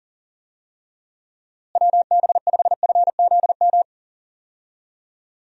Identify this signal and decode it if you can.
Morse code, heard as a keyed tone, about 37 wpm, 700 Hz. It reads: WB5FZM
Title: USA Call Signs - 6 Characters - Repeated 3x - 40wpm